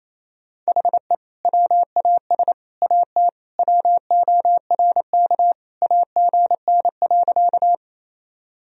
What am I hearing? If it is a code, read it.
Morse code, heard as a keyed tone, about 28 wpm, 700 Hz. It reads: HE WAS AT WORK AGN.
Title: Sentences from Top 200 Words and Abbreviations 28wpm